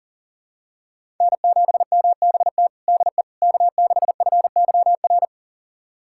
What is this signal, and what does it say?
Morse code, heard as a keyed tone, about 40 wpm, 700 Hz. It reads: N7MBT DE K6FYR
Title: USA Callsign DE Callsign - 5 Characters - 40wpm